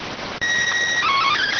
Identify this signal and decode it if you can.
SSTV (slow-scan television), PD 290.